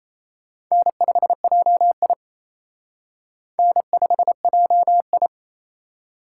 Morse code, heard as a keyed tone, about 35 wpm, 700 Hz. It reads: N5JI N5JI
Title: Speed Racing - USA Call Signs 28wpm